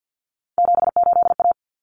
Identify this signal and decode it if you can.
Morse code, heard as a keyed tone, about 50 wpm, 700 Hz. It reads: 78A
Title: Common Cut Numbers - 3-Digit 50wpm